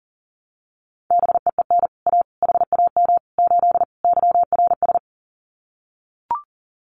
Morse code, heard as a keyed tone, about 40 wpm, 700 Hz. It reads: BEEN A HAM 8 YRS
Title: Common QSO Phrases Encore 40wpm